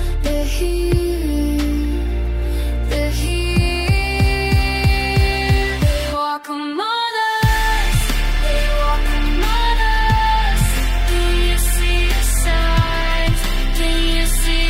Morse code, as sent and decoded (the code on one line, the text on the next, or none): none